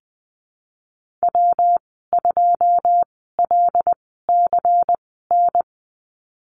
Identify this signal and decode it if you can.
Morse code, heard as a keyed tone, about 20 wpm, spaced as for 14 wpm, 700 Hz. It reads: W2LCN
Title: 122 Lesson 20g - Number 3 - Call Signs 20wpm